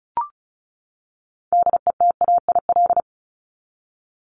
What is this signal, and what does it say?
Morse code, heard as a keyed tone, about 34 wpm, 700 Hz. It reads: DETAIL
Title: Top 1251-1500 Words 35wpm